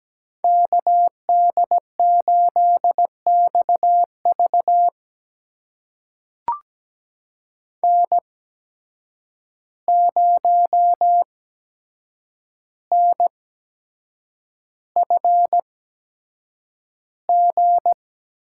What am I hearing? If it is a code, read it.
Morse code, heard as a keyed tone, about 17 wpm, spaced as for 7 wpm, 700 Hz. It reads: KD8XV N0NFG
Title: USA Call Signs - 5 Characters - 8x Character Spacing - 17wpm